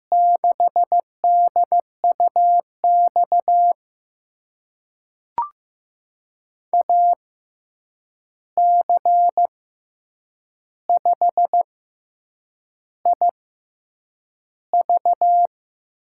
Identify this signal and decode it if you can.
Morse code, heard as a keyed tone, about 15 wpm, spaced as for 7 wpm, 700 Hz. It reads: 6DUX AC5IV
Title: USA Call Signs - 6x Character Spacing - 15wpm